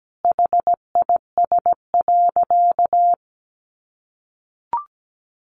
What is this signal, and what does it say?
Morse code, heard as a keyed tone, about 17 wpm, 700 Hz. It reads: HIS.
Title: Sentences from Top 900 Words 17wpm